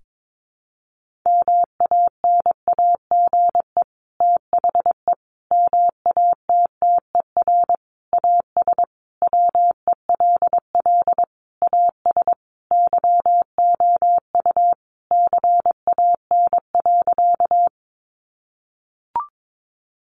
Morse code, heard as a keyed tone, about 22 wpm, 700 Hz. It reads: MANAGE THE MATTER AS WELL AS YOU CAN.
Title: Sentences from Top 1400 Words 22wpm